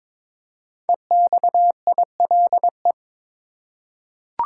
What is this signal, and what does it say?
Morse code, heard as a keyed tone, about 22 wpm, 700 Hz. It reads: EXILE
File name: 5 Letter Words - First 3 Letters Given - 22wpm